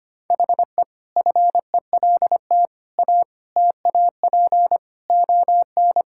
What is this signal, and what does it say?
Morse code, heard as a keyed tone, about 25 wpm, 700 Hz. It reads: HE FELT A TAP ON
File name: practice / 25wpm / Sentences with 4-Word Phrases - 25wpm